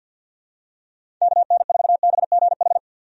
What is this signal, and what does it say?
Morse code, heard as a keyed tone, about 51 wpm, 700 Hz. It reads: KN4BGH
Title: USA Call Signs - 6 Characters - 50wpm